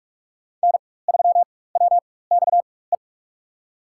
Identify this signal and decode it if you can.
Morse code, heard as a keyed tone, about 46 wpm, spaced as for 21 wpm, 700 Hz. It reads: N3WXE
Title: USA Call Signs - 5 Characters - 4x Character Spacing - 45wpm